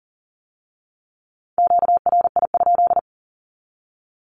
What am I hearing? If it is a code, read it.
Morse code, heard as a keyed tone, about 39 wpm, 700 Hz. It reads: QRI?
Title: Q-Codes with Spoken Meaning - 40wpm